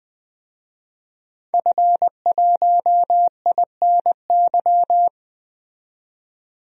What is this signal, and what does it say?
Morse code, heard as a keyed tone, about 20 wpm, 700 Hz. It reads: F1INY
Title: DXCC-100 Call Signs - 5 Characters - 20wpm